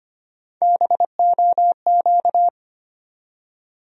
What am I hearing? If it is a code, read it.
Morse code, heard as a keyed tone, about 25 wpm, 700 Hz. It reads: BOQ
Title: USA Call Sign Postfixes - 3 Letters 25wpm